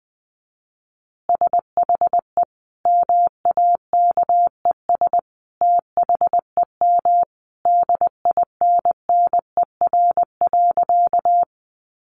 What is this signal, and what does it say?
Morse code, heard as a keyed tone, about 20 wpm, 700 Hz. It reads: SHE MAKES THEM DINNER.
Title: Easy Phrases 20wpm